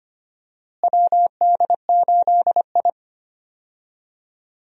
Morse code, heard as a keyed tone, about 25 wpm, 700 Hz. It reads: WD8I
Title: USA Call Signs - 4 Characters - Repeated 2x - 25wpm